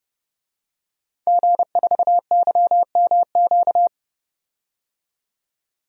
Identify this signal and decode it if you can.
Morse code, heard as a keyed tone, about 30 wpm, 700 Hz. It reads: G4YMQ
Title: DXCC-100 Call Signs - 5 Characters - 30wpm